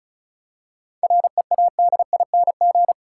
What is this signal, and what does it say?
Morse code, heard as a keyed tone, about 36 wpm, 700 Hz. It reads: READING
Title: Seven Letter Words - 35wpm